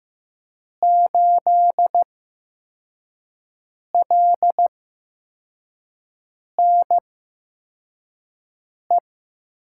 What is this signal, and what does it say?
Morse code, heard as a keyed tone, about 15 wpm, spaced as for 4 wpm, 700 Hz. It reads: 8LNE
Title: USA Call Signs - 5 Characters - 8x Character Spacing - 15wpm